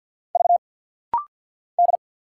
Morse code, heard as a keyed tone, about 48 wpm, 690 Hz. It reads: V D
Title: Single Letter-Number - Rapid-Fire 50wpm